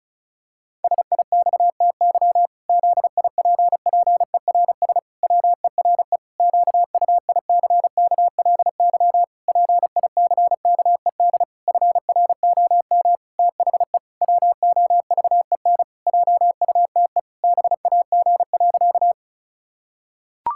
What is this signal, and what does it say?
Morse code, heard as a keyed tone, about 34 wpm, 700 Hz. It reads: SIXTY ZIPPERS WERE QUICKLY PICKED FROM THE WOVEN JUTE BAG.
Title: Pangram Challenge v2 - 35wpm